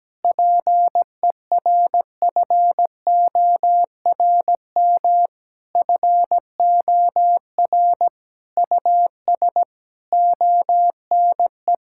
Morse code, heard as a keyed tone, about 17 wpm, 700 Hz. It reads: PERFORM FOR US ONE
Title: Sentences from Top 1000 Words - Review Part 5 - 17wpm